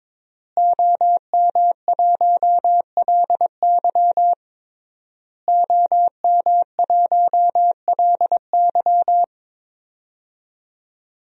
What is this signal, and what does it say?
Morse code, heard as a keyed tone, about 22 wpm, 700 Hz. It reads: OM1LY OM1LY
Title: DXCC-100 Call Signs - 5 Characters - Repeated 3x - 22wpm